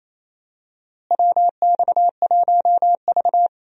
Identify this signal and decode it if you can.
Morse code, heard as a keyed tone, about 28 wpm, 700 Hz. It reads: WX1V
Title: USA Call Signs - 4 Characters - Repeated 3x - 28wpm